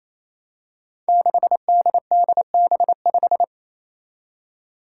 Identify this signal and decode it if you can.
Morse code, heard as a keyed tone, about 28 wpm, 700 Hz. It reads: 6DDB5
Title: Five Challenging Letter-Number Combinations 28wpm